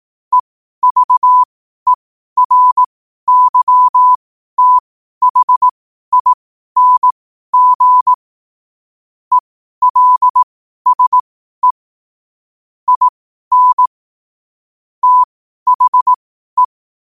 Morse code, heard as a keyed tone, about 18 wpm, spaced as for 12 wpm, 1000 Hz. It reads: EVERYTHING ELSE IN THE
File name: qotdfiles / 12wpm